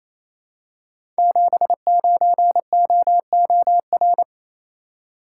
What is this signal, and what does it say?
Morse code, heard as a keyed tone, about 27 wpm, 700 Hz. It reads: 79OOR
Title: Code Groups - 5 Letter:Numbers - 28wpm